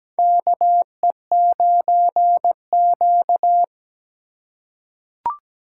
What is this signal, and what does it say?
Morse code, heard as a keyed tone, about 17 wpm, 700 Hz. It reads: KE9Q
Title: USA Call Signs - 4 Characters - 4x Character Spacing - 17wpm